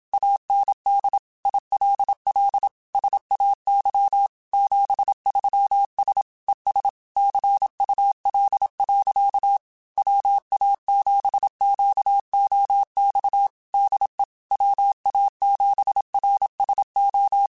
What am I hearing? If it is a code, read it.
Morse code, heard as a keyed tone, about 27 wpm, 800 Hz. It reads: AND ILL SAY 73S ES CUL. WA7QOX DE WA7RSO
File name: QSOs / QSOs-mp3s / WA7QOX 2@25